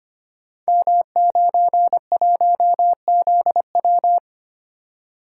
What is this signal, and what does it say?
Morse code, heard as a keyed tone, about 25 wpm, 700 Hz. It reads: M91ZW
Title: Code Groups - 5 Letter:Numbers - 25wpm